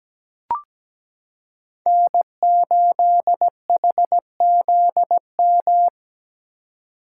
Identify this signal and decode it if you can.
Morse code, heard as a keyed tone, about 17 wpm, 700 Hz. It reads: N8HZM
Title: K1USN SST QSO Practice - 17wpm